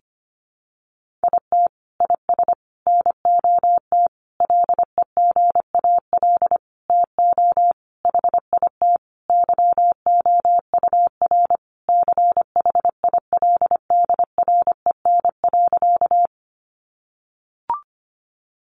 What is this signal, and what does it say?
Morse code, heard as a keyed tone, about 25 wpm, 700 Hz. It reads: IT IS NOT LEGAL TO HIT YOUR CHILDREN.
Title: Sentences from Top 1200 Words 25wpm